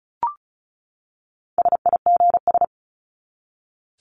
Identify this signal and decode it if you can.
Morse code, heard as a keyed tone, about 34 wpm, 700 Hz. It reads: SIGS
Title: Abbreviations 35wpm